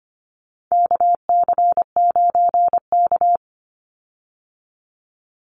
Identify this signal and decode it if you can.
Morse code, heard as a keyed tone, about 25 wpm, 700 Hz. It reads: KC9K
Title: Call Signs Repeated 2x - 25wpm